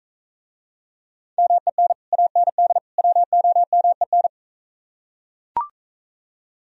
Morse code, heard as a keyed tone, about 42 wpm, 700 Hz. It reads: MEN AND WOMEN